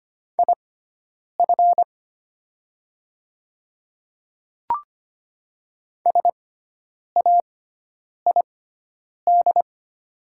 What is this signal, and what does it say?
Morse code, heard as a keyed tone, about 25 wpm, spaced as for 9 wpm, 700 Hz. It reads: IF SAID
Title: 6x Character Spacing - Top 100 Words - 25wpm